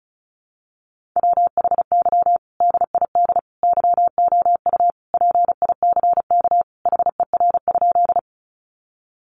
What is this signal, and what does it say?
Morse code, heard as a keyed tone, about 34 wpm, 700 Hz. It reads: WHY DID YOU PICK HER?
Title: Sentences from Top 1000 Words 35wpm